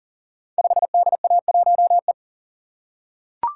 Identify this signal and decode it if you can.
Morse code, heard as a keyed tone, about 42 wpm, 700 Hz. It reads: 5DA1E
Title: Code Groups - 5 Letter:Numbers - 40wpm